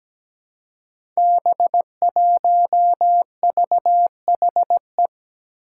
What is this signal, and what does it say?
Morse code, heard as a keyed tone, about 17 wpm, 700 Hz. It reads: B1VHE
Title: Five Challenging Letter-Number Combinations 17wpm